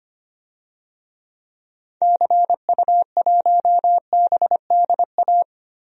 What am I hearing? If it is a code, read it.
Morse code, heard as a keyed tone, about 25 wpm, 700 Hz. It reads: CU1BDA
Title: DXCC-100 Call Signs - 6 Characters - Repeated 2x - 25wpm